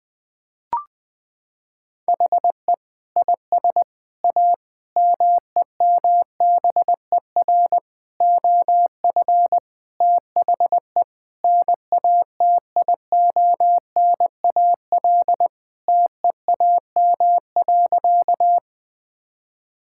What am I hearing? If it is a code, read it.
Morse code, heard as a keyed tone, about 20 wpm, 700 Hz. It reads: HE IS A MEMBER OF THE NATIONAL TEAM.
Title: Sentences with 3-Word Phrases - 20wpm